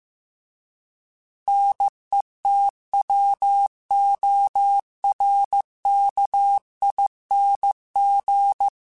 Morse code, heard as a keyed tone, about 15 wpm, 800 Hz. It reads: NETWORKING